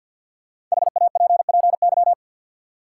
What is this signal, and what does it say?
Morse code, heard as a keyed tone, about 53 wpm, 700 Hz. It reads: HAPPY
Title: Sentences from Top 1000 Words - New Word Drill - 50wpm